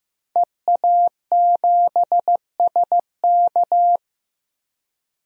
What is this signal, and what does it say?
Morse code, heard as a keyed tone, about 15 wpm, 700 Hz. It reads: EA7SK
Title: DXCC-100 Call Signs - 5 Characters - Repeated 3x - 15wpm